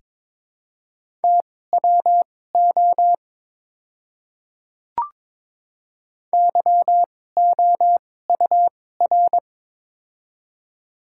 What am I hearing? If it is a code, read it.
Morse code, heard as a keyed tone, about 22 wpm, spaced as for 16 wpm, 700 Hz. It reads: TWO YOUR
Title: 2x Character Spacing - Top 100 Words - 22wpm